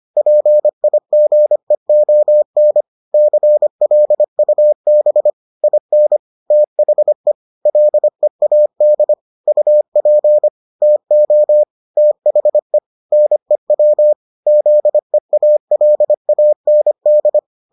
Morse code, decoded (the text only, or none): PIGEON CLUB IN THE LEAD UP TO THE NEW ZEALAND